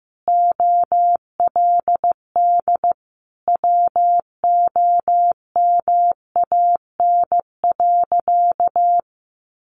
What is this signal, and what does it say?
Morse code, heard as a keyed tone, about 15 wpm, 700 Hz. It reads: OLD WOMAN.
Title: Sentences from Top 1100 Words 15wpm